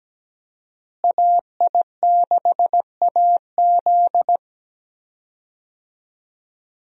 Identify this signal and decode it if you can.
Morse code, heard as a keyed tone, about 17 wpm, 700 Hz. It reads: AI6AZ
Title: USA Call Signs - 5 Characters - Repeated 2x - 17wpm